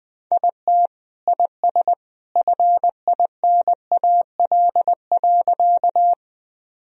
Morse code, heard as a keyed tone, about 20 wpm, 700 Hz. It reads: IT IS FINAL.
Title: Sentences from Top 1000 Words - Review Part 2 - 20wpm